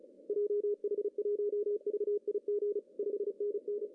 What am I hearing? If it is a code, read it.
Morse code, heard as a keyed tone, about 30 wpm, 410 Hz. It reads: JH1VIG 5NN
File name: CW / cqww-cw-k5tr